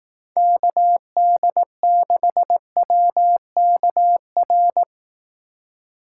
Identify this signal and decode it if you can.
Morse code, heard as a keyed tone, about 18 wpm, 700 Hz. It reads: KD6WKR
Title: Speed Racing - USA 2x3 Call Signs 15wpm